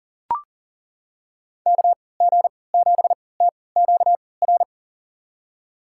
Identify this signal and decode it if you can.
Morse code, heard as a keyed tone, about 40 wpm, spaced as for 23 wpm, 700 Hz. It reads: KG7TQR
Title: USA Call Signs - 3x Character Spacing - 40wpm